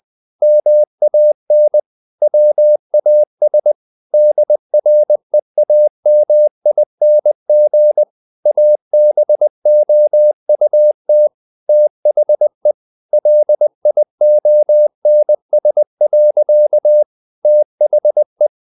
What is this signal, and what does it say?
Morse code, heard as a keyed tone, about 20 wpm, 600 Hz. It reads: MAN WAS DREAMING ABOUT THE LIONS. THE
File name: the old man and the sea part 2 20wpm